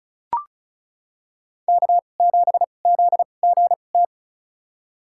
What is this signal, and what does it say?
Morse code, heard as a keyed tone, about 36 wpm, spaced as for 25 wpm, 700 Hz. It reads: K7ZGT
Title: USA Call Signs - 5 Characters - 2x Character Spacing - 35wpm